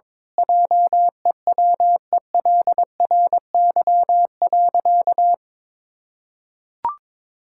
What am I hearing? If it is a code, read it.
Morse code, heard as a keyed tone, about 22 wpm, 700 Hz. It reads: JEWELRY.
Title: Pangram Challenge v2 - 22wpm